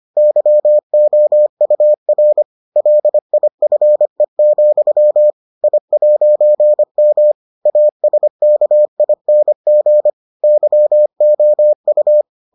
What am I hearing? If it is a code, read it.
Morse code, heard as a keyed tone, about 25 wpm, 600 Hz. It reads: YOUR LIFE, I'M ASKING YOU